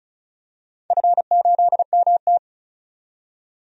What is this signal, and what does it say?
Morse code, heard as a keyed tone, about 36 wpm, 700 Hz. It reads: F8MT
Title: DXCC-100 Call Signs - 4 Characters - 35wpm